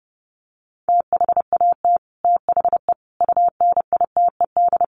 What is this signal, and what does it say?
Morse code, heard as a keyed tone, about 30 wpm, 700 Hz. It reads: THAT THE UNITED
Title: Sets of 5 Words Encore 30wpm